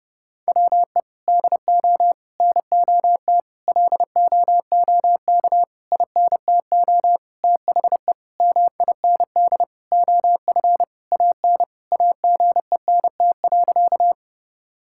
Morse code, heard as a keyed tone, about 30 wpm, 700 Hz. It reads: WE DO NOT LOOK INTO THE MIND OF AN AGENT.
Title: Sentences from Top 1600 Words 30wpm